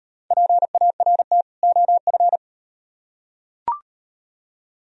Morse code, heard as a keyed tone, about 38 wpm, 700 Hz. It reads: PART OF